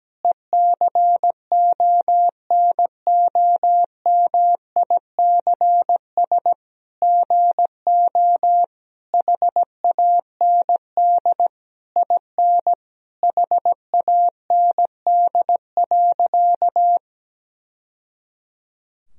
Morse code, heard as a keyed tone, about 17 wpm, 700 Hz. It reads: ECONOMICS GO HAND IN HAND.
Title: Sentences from Top 1000 Words - Review Part 3 - 17wpm